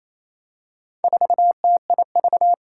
Morse code, heard as a keyed tone, about 28 wpm, 700 Hz. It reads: 4TIV
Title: Four Challenging Letter-Number Combinations 28wpm